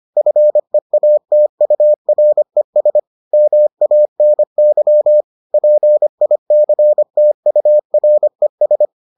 Morse code, heard as a keyed tone, about 25 wpm, 600 Hz. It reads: FEATURES MANY PICTURES